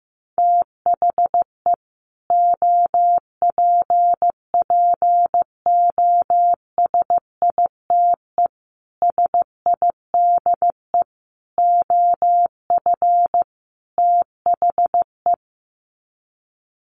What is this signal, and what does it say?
Morse code, heard as a keyed tone, about 15 wpm, 700 Hz. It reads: THE OPPOSITE SIDE OF THE
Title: Sets of 5 Words Encore 15wpm